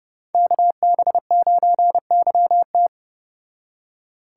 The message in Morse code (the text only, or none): KB9YT